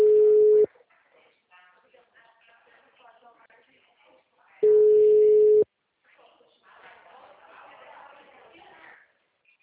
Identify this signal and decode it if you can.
Ringback tone (Europe, 425 Hz), with speech.